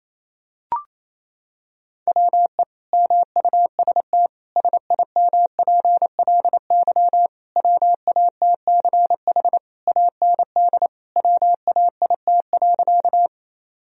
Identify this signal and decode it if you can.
Morse code, heard as a keyed tone, about 28 wpm, 700 Hz. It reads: WE MUST SIMPLY WATCH AND WAIT.
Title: Sentences from Top 1000 Words - Review Part 4 - 28wpm